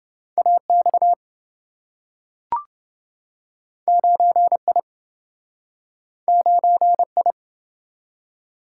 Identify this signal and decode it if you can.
Morse code, heard as a keyed tone, about 27 wpm, 700 Hz. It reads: AX 9I 9I